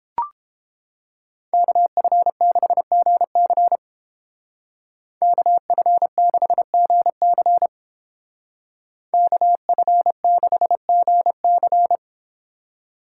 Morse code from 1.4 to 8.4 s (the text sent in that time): KF6GC KF6GC